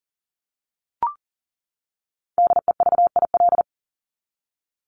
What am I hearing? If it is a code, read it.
Morse code, heard as a keyed tone, about 40 wpm, 700 Hz. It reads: DEVIL
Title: Top 1751-2000 Words 40wpm